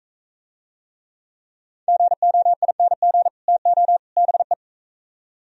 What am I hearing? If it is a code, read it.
Morse code, heard as a keyed tone, about 44 wpm, 700 Hz. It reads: GOING TO BE